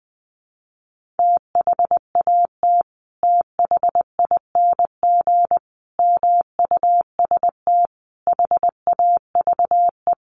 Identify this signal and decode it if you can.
Morse code, heard as a keyed tone, about 20 wpm, 700 Hz. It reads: THAT THING MUST HAVE
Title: Sentences from Top 1500 Words 20wpm